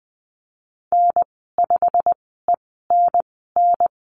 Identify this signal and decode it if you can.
Morse code, heard as a keyed tone, about 20 wpm, spaced as for 14 wpm, 700 Hz. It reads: N5ENN
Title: 128 Lesson 21g - Number 6 - Call Signs 20wpm